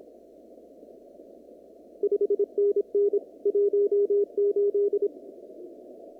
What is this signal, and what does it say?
Morse code, heard as a keyed tone, about 27 wpm, 410 Hz. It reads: 5NN 18
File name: CW / kt5j wpx cw-op-k5pi